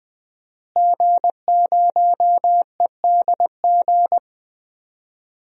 Morse code, heard as a keyed tone, about 20 wpm, 700 Hz. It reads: G0EDG
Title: DXCC-100 Call Signs - 5 Characters - Repeated 2x - 20wpm